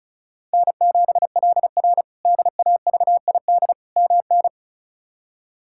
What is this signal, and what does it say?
Morse code, heard as a keyed tone, about 34 wpm, 700 Hz. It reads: N7LR DAVID MN
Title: K1USN SST QSO Practice - 35wpm